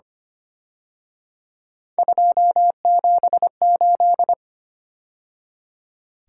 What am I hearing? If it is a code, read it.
Morse code, heard as a keyed tone, about 31 wpm, 700 Hz. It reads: 278
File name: Speed-Racing - Three Letter-Number Combinations - 25wpm